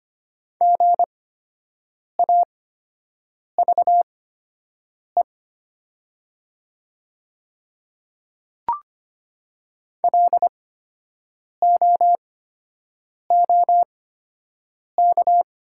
Morse code, heard as a keyed tone, about 25 wpm, spaced as for 7 wpm, 700 Hz. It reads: GAVE LOOK
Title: 8x Character Spacing - Top 500 Words - 25wpm